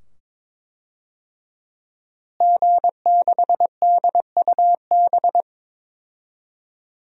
Morse code, heard as a keyed tone, about 22 wpm, 700 Hz. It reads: G6DUB
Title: DXCC-100 Call Signs - 5 Characters - Repeated 2x - 22wpm